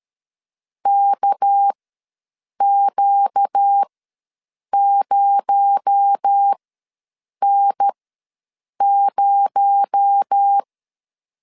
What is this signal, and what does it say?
Morse code, heard as a keyed tone, about 13 wpm, spaced as for 7 wpm, 800 Hz. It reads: KQ0N0
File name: LEZIONI / 6 PKX0 / 6 PKX0 ws 12wpm 3448